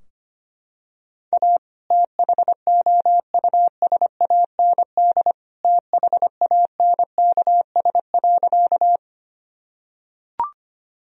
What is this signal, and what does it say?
Morse code, heard as a keyed tone, about 25 wpm, 700 Hz. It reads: A THOUSAND THANKS.
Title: Sentences from Top 1000 Words - Review Part 3 - 25wpm